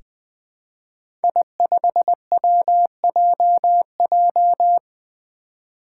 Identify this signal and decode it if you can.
Morse code, heard as a keyed tone, about 20 wpm, 700 Hz. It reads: I5WJJ
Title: Five Challenging Letter-Number Combinations 20wpm